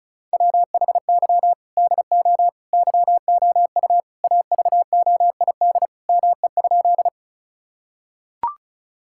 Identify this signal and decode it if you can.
Morse code, heard as a keyed tone, about 36 wpm, 700 Hz. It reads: WHY DO YOU AVOID ME?
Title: Sentences from Top 1000 Words - Review Part 6 - 35wpm